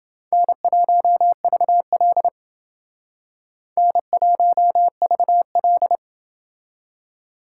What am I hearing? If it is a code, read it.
Morse code, heard as a keyed tone, about 28 wpm, 700 Hz. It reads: N1VL N1VL